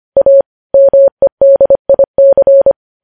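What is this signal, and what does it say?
Morse code, heard as a keyed tone, about 25 wpm, 550 Hz. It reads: A MEDIC
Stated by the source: Morse code practice of news headlines sent at 25 wpm.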